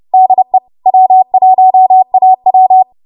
Morse code, heard as a keyed tone, about 30 wpm, 750 Hz. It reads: DE W1AW
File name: Archive / 30 WPM / 30WPM